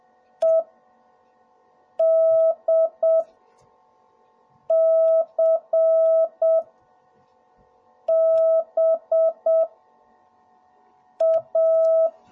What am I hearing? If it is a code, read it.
Morse code, heard as a keyed tone, about 7 wpm, spaced as for 4 wpm, 650 Hz. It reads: EDCBA